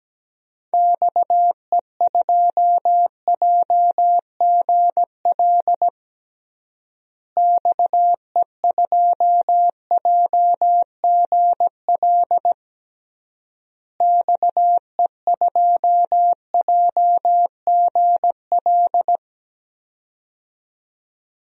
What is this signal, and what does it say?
Morse code, heard as a keyed tone, about 17 wpm, 700 Hz. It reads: XE2JGL XE2JGL XE2JGL
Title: DXCC-100 Call Signs - 6 Characters - Repeated 3x - 17wpm